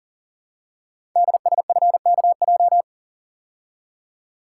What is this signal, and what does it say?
Morse code, heard as a keyed tone, about 42 wpm, 700 Hz. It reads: DSFKJ
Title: Code Groups - 5 Letters - 40wpm